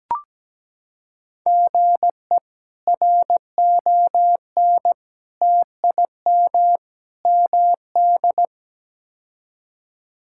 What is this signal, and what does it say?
Morse code, heard as a keyed tone, about 17 wpm, 700 Hz. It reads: GE RON TIM MD
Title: K1USN SST QSO Practice - 17wpm